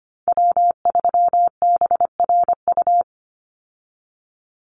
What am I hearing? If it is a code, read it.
Morse code, heard as a keyed tone, about 25 wpm, 700 Hz. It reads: W3BRU
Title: USA Call Signs 25wpm